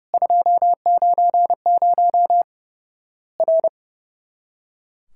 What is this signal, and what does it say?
Morse code, heard as a keyed tone, about 30 wpm, 700 Hz. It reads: 290 R
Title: Worked All Europe DX Contest - QTC Exchange - 30wpm